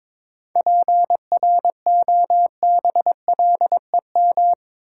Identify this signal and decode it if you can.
Morse code, heard as a keyed tone, about 22 wpm, 700 Hz. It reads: PROBLEM
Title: Sentences from Top 1000 Words - New Word Drill - 22wpm